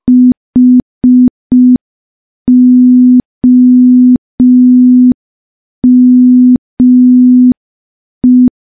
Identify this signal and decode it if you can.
Morse code, heard as a keyed tone, about 5 wpm, 250 Hz. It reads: HOME
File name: morse / morse-training-05WPM